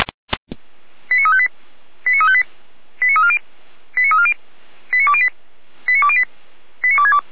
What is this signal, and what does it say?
Five-tone selective call (ZVEI), heard as quick runs of five tones, tones of about 70 ms. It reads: 89358 89358 89340 89340 89208 89208 89262